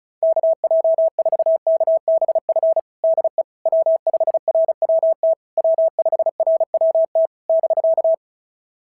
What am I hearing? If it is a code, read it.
Morse code, heard as a keyed tone, about 36 wpm, 660 Hz. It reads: KJ4KBF DE W5RWT W5RWT <BK>
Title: SOTA QSO Practice 35wpm